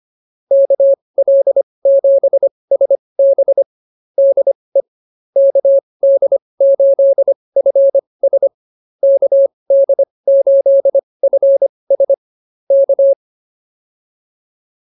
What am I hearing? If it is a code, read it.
Morse code, heard as a keyed tone, about 24 wpm, spaced as for 20 wpm, 550 Hz. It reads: KL7SB DE KD8FS KD8FS K